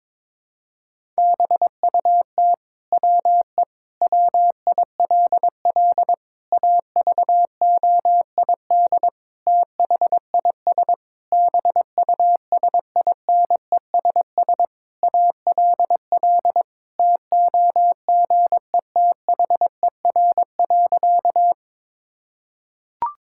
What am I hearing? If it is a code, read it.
Morse code, heard as a keyed tone, about 22 wpm, 700 Hz. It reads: BUT WE WILL AVOID THIS BUSINESS ALL TOGETHER.
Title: Sentences from Top 1000 Words - Review Part 6 - 22wpm